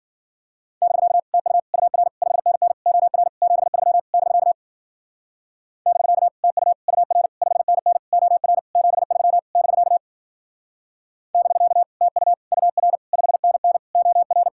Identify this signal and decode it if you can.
Morse code, heard as a keyed tone, about 52 wpm, 700 Hz. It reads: <BK> TU UR 5NN OR 73 <BK> <BK> TU UR 5NN OR 73 <BK> <BK> TU UR 5NN OR